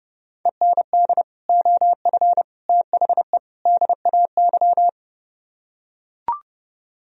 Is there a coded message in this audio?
Yes: Morse code from 0.5 to 4.9 s, heard as a keyed tone, about 30 wpm, 700 Hz. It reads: END OF THE DAY